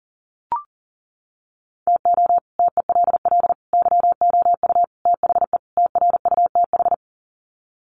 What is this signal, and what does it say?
Morse code, heard as a keyed tone, about 40 wpm, 700 Hz. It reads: TO TELL YOU THE TRUTH
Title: Sets of 5 Words 40wpm